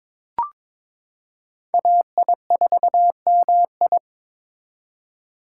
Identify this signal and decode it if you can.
Morse code, heard as a keyed tone, about 22 wpm, 700 Hz. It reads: AI4MI
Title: K1USN SST QSO Practice - 22wpm